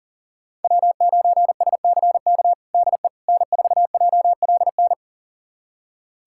Morse code, heard as a keyed tone, about 40 wpm, 700 Hz. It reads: W9SCK DE N4JLN
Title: USA Callsign DE Callsign - 5 Characters - 40wpm